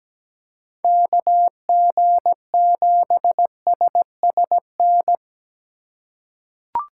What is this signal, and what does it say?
Morse code, heard as a keyed tone, about 17 wpm, 700 Hz. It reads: KG7SSN
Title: USA Call Signs - 6 Characters - 3x Character Spacing - 17wpm